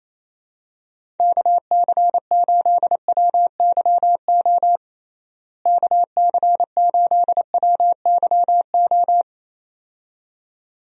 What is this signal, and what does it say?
Morse code, heard as a keyed tone, about 28 wpm, 700 Hz. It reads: KC8WYO KC8WYO
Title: USA Call Signs - 6 Characters - Repeated 2x - 28wpm